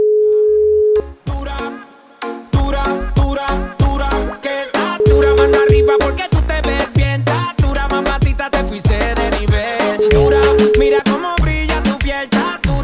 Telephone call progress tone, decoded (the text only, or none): ringback tone